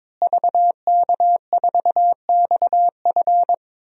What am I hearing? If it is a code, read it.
Morse code, heard as a keyed tone, about 22 wpm, 700 Hz. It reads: VK4XF
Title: DXCC-100 Call Signs - 5 Characters - Repeated 2x - 22wpm